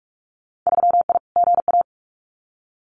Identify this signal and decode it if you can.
Morse code, heard as a keyed tone, about 46 wpm, 700 Hz. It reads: 3I GA
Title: Winter Field Day Exchanges 45wpm